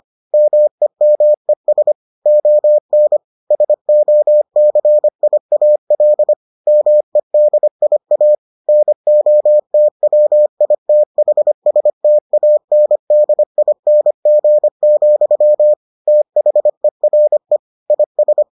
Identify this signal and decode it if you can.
Morse code, heard as a keyed tone, about 25 wpm, 600 Hz. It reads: MEMES ON SOCIAL MEDIA NOTWITHSTANDING, THERE IS